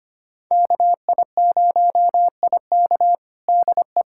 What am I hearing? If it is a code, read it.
Morse code, heard as a keyed tone, about 25 wpm, 700 Hz. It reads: KI0IK DE
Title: USA Callsign DE Callsign - 5 Characters - 25wpm